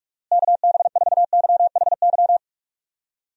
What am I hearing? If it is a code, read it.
Morse code, heard as a keyed tone, about 46 wpm, 700 Hz. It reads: KB4YHY
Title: USA Call Signs - 6 Characters - Repeated 3x - 45wpm